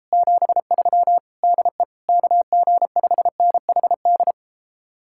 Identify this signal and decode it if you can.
Morse code, heard as a keyed tone, about 33 wpm, 700 Hz. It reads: 73 DE KG5NHD